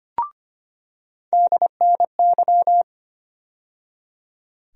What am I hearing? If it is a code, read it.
Morse code, heard as a keyed tone, about 25 wpm, 700 Hz. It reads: DNY
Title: USA Call Sign Postfixes - 3 Letters 25wpm